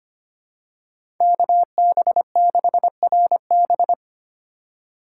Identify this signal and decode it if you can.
Morse code, heard as a keyed tone, about 25 wpm, 700 Hz. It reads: KB6RB
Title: USA Call Signs - 5 Characters - 4x Character Spacing - 25wpm